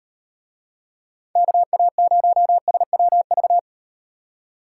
Morse code, heard as a keyed tone, about 38 wpm, 700 Hz. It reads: KA0SWV